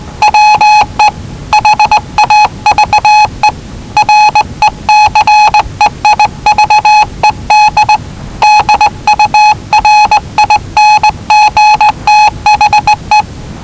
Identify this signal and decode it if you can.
Morse code, heard as a keyed tone, about 19 wpm, 850 Hz. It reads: WE HAVE RECEIVED DURING THE
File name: WellingtonRadioZlwCloseDownOnHfAndMf / 01-Zlw 4277khz 417.5 qrt until 0925z